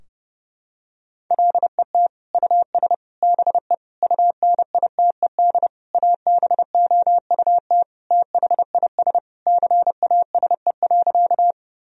Morse code, heard as a keyed tone, about 30 wpm, 700 Hz. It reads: LET US BE UNITED ABOUT THIS CASE.
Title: Sentences from Top 1000 Words - Review Part 4 - 30wpm